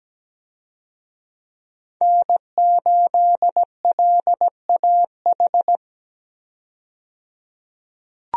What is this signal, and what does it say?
Morse code, heard as a keyed tone, about 17 wpm, 700 Hz. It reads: N8LAH
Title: USA Call Signs - 5 Characters - 17wpm